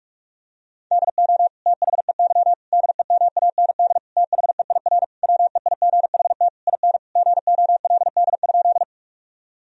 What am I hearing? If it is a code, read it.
Morse code, heard as a keyed tone, about 46 wpm, 700 Hz. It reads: DO THEY DEMAND THEIR WEIGHT IN GOLD?
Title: Sentences from Top 1000 Words - Review Part 3 - 45wpm